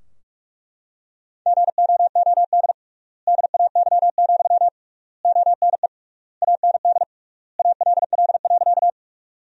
Morse code, heard as a keyed tone, about 42 wpm, 700 Hz. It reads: GOOD DAY, ONE AND ALL.
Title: 3x Word-Spacing - Sentences from Top 200 Words - 45wpm